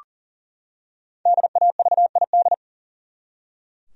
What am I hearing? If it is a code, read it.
Morse code, heard as a keyed tone, about 40 wpm, 700 Hz. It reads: DAVID
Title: USA Common Names - Max 5 Letters - 40wpm